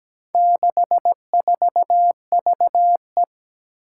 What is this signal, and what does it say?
Morse code, heard as a keyed tone, about 17 wpm, 700 Hz. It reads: 64VE
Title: Four Challenging Letter-Number Combinations 17wpm